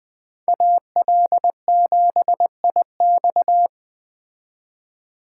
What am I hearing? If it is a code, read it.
Morse code, heard as a keyed tone, about 21 wpm, 700 Hz. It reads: AL7IX